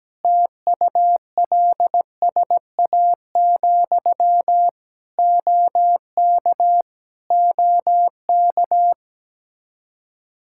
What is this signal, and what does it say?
Morse code, heard as a keyed tone, about 17 wpm, 700 Hz. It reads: TULSA, OK OK
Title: Practice QSO - Exchange 1 - Beginning - 17wpm